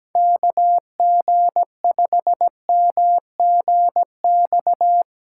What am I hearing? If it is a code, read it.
Morse code, heard as a keyed tone, about 17 wpm, 700 Hz. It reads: KG5MGX
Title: USA Call Signs - 6 Characters - 4x Character Spacing - 17wpm